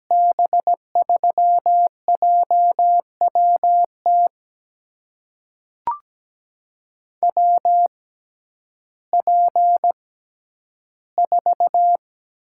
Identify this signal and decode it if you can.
Morse code, heard as a keyed tone, about 17 wpm, spaced as for 10 wpm, 700 Hz. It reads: B3JWT WP4
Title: USA Call Signs - 6 Characters - 6x Character Spacing - 17wpm